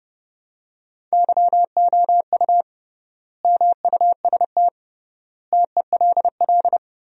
Morse code, heard as a keyed tone, about 31 wpm, 700 Hz. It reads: YOU MUST TELL
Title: 3x Word-Spacing - Sentences from Top 200 Words - 30wpm